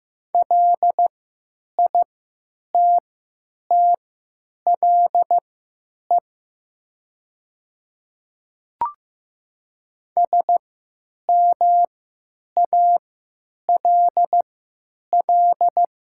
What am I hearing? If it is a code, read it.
Morse code, heard as a keyed tone, about 15 wpm, spaced as for 8 wpm, 700 Hz. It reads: LITTLE SMALL
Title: 3x Character Spacing - Top 200 Words - 15wpm